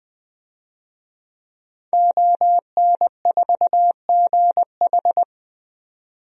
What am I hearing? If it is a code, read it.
Morse code, heard as a keyed tone, about 20 wpm, 700 Hz. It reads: ON4GH
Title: DXCC-100 Call Signs - 5 Characters - 20wpm